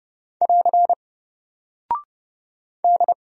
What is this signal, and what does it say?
Morse code, heard as a keyed tone, about 30 wpm, 700 Hz. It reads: <AR> D
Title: Single Character (Letter:Number:Punctuation:Prosign) Alt 30wpm